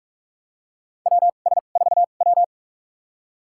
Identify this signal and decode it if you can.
Morse code, heard as a keyed tone, about 43 wpm, spaced as for 31 wpm, 700 Hz. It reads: WS4W
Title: USA Call Signs - 2x Character Spacing - 45wpm